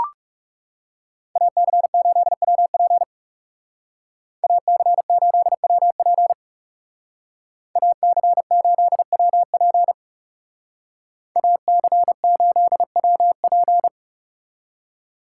Morse code, heard as a keyed tone, about 37 wpm, 700 Hz. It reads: AC8WP AC8WP AC8WP AC8WP
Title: Speed Racing - USA 2x2 Call Signs 30wpm